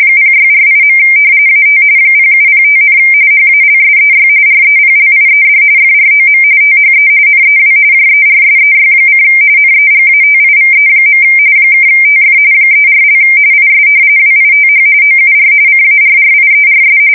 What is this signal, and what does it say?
RTTY, about 45.45 baud. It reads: 0648. AT 0705, HALSEY REPLIED FROM THE NEW JERSEY: NEGATIVE. (TF 34) IS WITH OUR CARRIERS
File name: fsk170 / battleshipnj